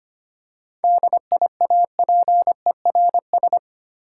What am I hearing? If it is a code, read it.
Morse code, heard as a keyed tone, about 25 wpm, 690 Hz. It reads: DIAPERS
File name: practice / 25wpm / Easy Words with Hints - 25wpm